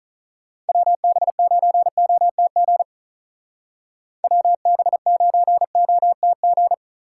Morse code, heard as a keyed tone, about 42 wpm, 700 Hz. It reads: WB9OTG WB9OTG
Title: Speed Racing - USA 2x3 Call Signs 35wpm